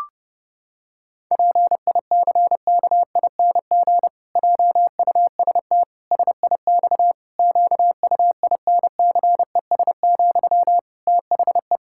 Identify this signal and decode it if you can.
Morse code, heard as a keyed tone, about 30 wpm, 700 Hz. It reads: PICKING JUST SIX QUINCES, THE
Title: Pangram Challenge v2 - 30wpm